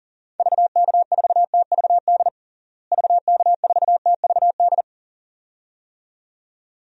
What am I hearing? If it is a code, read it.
Morse code, heard as a keyed tone, about 37 wpm, 700 Hz. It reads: VK4TVD VK4TVD
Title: DXCC-100 Call Signs - 6 Characters - Repeated 2x - 40wpm